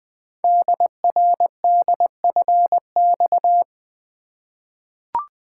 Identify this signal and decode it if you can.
Morse code, heard as a keyed tone, about 20 wpm, 700 Hz. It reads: DRDFX
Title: Code Groups - 5 Letters - 20wpm